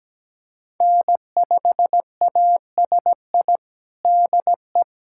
Morse code, heard as a keyed tone, about 17 wpm, 700 Hz. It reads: N5ASI DE
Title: USA Callsign DE Callsign 17wpm